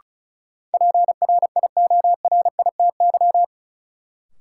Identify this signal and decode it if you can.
Morse code, heard as a keyed tone, about 36 wpm, 700 Hz. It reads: PRIORITY
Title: Sentences from Top 1600 Words - New Word Drill - 35wpm